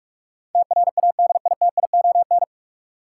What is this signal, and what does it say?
Morse code, heard as a keyed tone, about 46 wpm, 700 Hz. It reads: TRADITION
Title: Sentences from Top 1500 Words - New Word Drill Alt - 45wpm